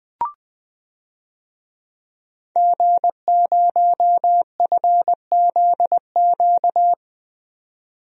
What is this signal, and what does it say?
Morse code, heard as a keyed tone, about 20 wpm, 700 Hz. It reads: G0FZQ
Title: DXCC-100 Call Signs - 5 Characters - 20wpm